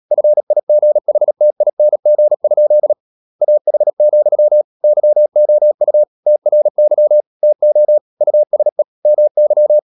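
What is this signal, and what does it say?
Morse code, heard as a keyed tone, about 37 wpm, 600 Hz. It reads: FIGHTING? AH, YOU TRY TO USE MY
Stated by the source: Morse code practice at 35–39 wpm sending English text of Ken Follett's novel Never, from CW med Gnister.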